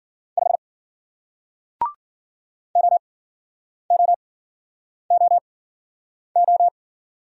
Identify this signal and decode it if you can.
Morse code, heard as a keyed tone, about 55 wpm, spaced as for 9 wpm, 700 Hz. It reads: 5 OOOO